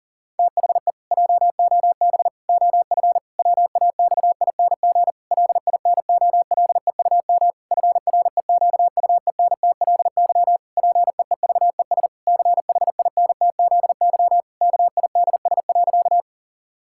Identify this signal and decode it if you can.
Morse code, heard as a keyed tone, about 40 wpm, 700 Hz. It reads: THE JOB OF WAXING LINOLEUM FREQUENTLY PEEVES CHINTZY KIDS.
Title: Pangram Challenge v2 - 40wpm